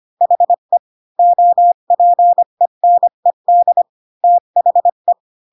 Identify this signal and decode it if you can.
Morse code, heard as a keyed tone, about 25 wpm, 700 Hz. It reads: HE OPENED THE
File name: Pinocchio / Pinocchio ch1 20 25 ik1hge